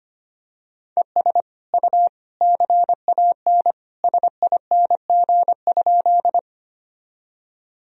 Morse code, heard as a keyed tone, about 25 wpm, 700 Hz. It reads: ES U CAN SING?
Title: Sentences from Top 400 Words and Abbreviations 25wpm